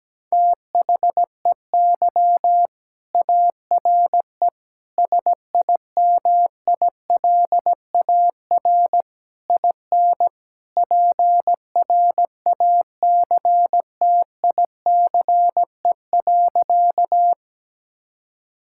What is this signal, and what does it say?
Morse code, heard as a keyed tone, about 17 wpm, 700 Hz. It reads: THEY ARE SIMILAR IN PRACTICE.
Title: Sentences from Top 1000 Words - Review Part 5 - 17wpm